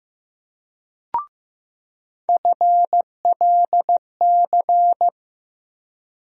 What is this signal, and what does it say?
Morse code, heard as a keyed tone, about 15 wpm, 700 Hz. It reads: FLC
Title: USA Call Sign Postfixes - 3 Letters 15wpm